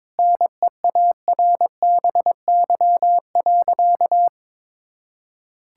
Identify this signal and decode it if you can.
Morse code, heard as a keyed tone, about 22 wpm, 700 Hz. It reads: NEARBY.
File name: practice / 22wpm / Sentences with 2-Word Phrases - 22wpm